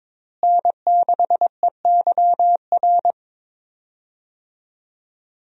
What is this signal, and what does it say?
Morse code, heard as a keyed tone, about 21 wpm, 700 Hz. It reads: N6EYR
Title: USA Call Signs - 5 Characters - Repeated 2x - 22wpm